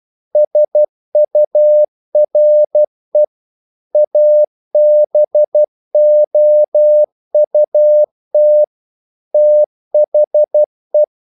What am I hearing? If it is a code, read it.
Morse code, heard as a keyed tone, about 12 wpm, 600 Hz. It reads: SURE ABOUT THE
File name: CW / Practice / EnglishQuotations0112